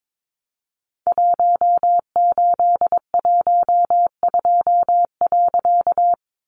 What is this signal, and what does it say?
Morse code, heard as a keyed tone, about 22 wpm, 700 Hz. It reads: 1812.
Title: A Sound Of Thunder 22wpm